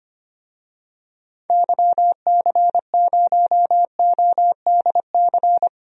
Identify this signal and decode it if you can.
Morse code, heard as a keyed tone, about 25 wpm, 700 Hz. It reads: YC0ODC
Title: DXCC-100 Call Signs - 6 Characters - 25wpm